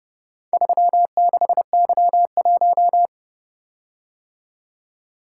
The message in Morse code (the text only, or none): 36Y1